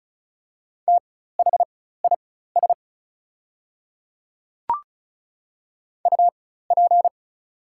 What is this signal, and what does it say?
Morse code, heard as a keyed tone, about 36 wpm, spaced as for 16 wpm, 700 Hz. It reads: THIS UP
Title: 4x Character Spacing - Top 300 Words - 35wpm